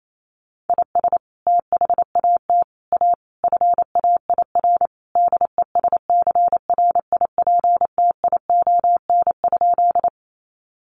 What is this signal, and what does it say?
Morse code, heard as a keyed tone, about 28 wpm, 700 Hz. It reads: IS THAT A FAIR DESCRIPTION?
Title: Sentences from Top 1300 Words 28wpm